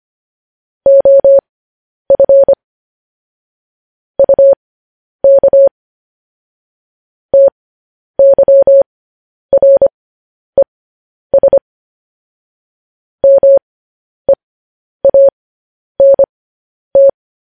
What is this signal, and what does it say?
Morse code, heard as a keyed tone, about 25 wpm, spaced as for 10 wpm, 550 Hz. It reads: OF UK TYRES MEANT
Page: news-headlines-in-morse-code-at-10-wpm